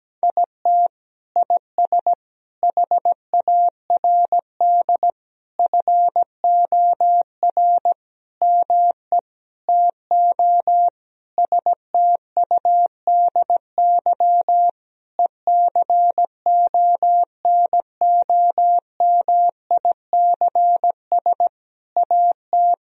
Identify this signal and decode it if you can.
Morse code, heard as a keyed tone, about 17 wpm, 700 Hz. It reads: IT IS HARD FOR ME TO STUDY ECONOMICS AT
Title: Sentences from Top 1000 Words - Review Part 3 - 17wpm